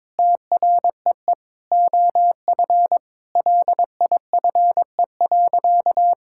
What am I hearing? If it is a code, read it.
Morse code, heard as a keyed tone, about 22 wpm, 700 Hz. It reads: TREE OF LIFE.
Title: Sentences from Top 1000 Words - Review Part 6 - 22wpm